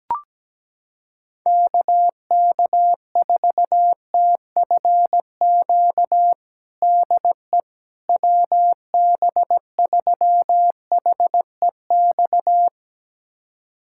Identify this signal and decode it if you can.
Morse code, heard as a keyed tone, about 17 wpm, 700 Hz. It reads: KK4TFQ DE WB3HEX
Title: USA Callsign DE Callsign - 6 Characters - 17wpm